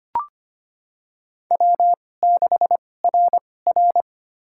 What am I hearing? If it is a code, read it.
Morse code, heard as a keyed tone, about 25 wpm, spaced as for 18 wpm, 700 Hz. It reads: W6RR
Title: USA Call Signs - 4 Characters - 2x Character Spacing - 25wpm